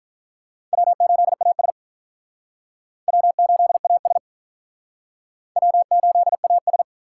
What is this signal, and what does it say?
Morse code, heard as a keyed tone, about 46 wpm, 700 Hz. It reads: W8AS W8AS W8AS